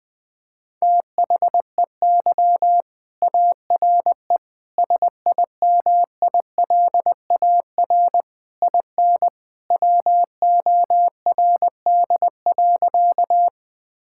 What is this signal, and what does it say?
Morse code, heard as a keyed tone, about 21 wpm, 700 Hz. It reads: THEY ARE SIMILAR IN WORD.
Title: Sentences from Top 1000 Words - Review Part 5 - 20wpm